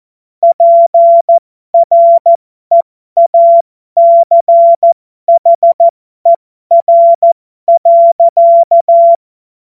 Morse code, heard as a keyed tone, about 14 wpm, spaced as for 12 wpm, 680 Hz. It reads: PREACHER.
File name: Morsekurs / Witze